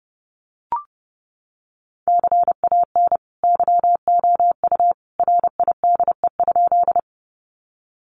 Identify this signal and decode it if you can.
Morse code, heard as a keyed tone, about 30 wpm, 700 Hz. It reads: CAN YOU RIDE?
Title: Sentences from Top 1200 Words 30wpm